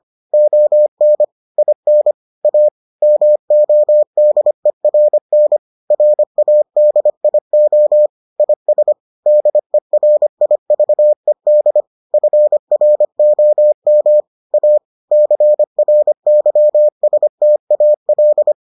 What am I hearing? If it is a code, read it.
Morse code, heard as a keyed tone, about 25 wpm, 600 Hz. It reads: ON IN A MODERN RADIO IS DERIVED FROM A CRYSTAL